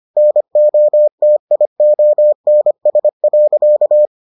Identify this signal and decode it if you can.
Morse code, heard as a keyed tone, about 25 wpm, 600 Hz. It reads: NOTIONS.